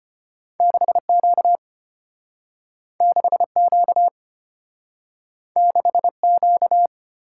Morse code, heard as a keyed tone, about 34 wpm, 690 Hz. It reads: 6Q 6Q 6Q